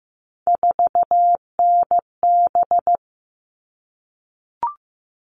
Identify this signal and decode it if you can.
Morse code, heard as a keyed tone, about 15 wpm, 700 Hz. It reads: 4NB
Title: All Cut Numbers - 3-Digit 15wpm